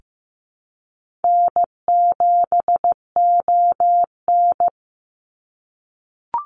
Morse code, heard as a keyed tone, about 15 wpm, 700 Hz. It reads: N7ON
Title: Easy Call Signs - 15wpm